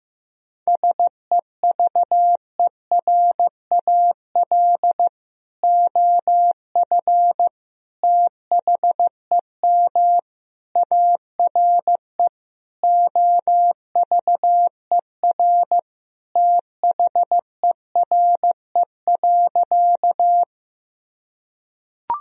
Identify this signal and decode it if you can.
Morse code, heard as a keyed tone, about 15 wpm, 700 Hz. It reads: SEVERAL OF THEM ARE OVER THERE.
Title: Sentences from Top 1000 Words - Review Part 2 - 15wpm